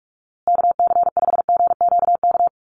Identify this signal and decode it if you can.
Morse code, heard as a keyed tone, about 44 wpm, 700 Hz. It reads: KC5GQK
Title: USA Call Signs Repeated 2x 45wpm